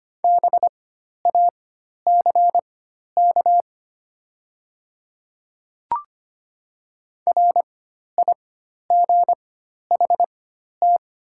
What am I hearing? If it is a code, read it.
Morse code, heard as a keyed tone, about 25 wpm, spaced as for 12 wpm, 700 Hz. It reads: BACK RIGHT
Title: 4x Character Spacing - Top 200 Words - 25wpm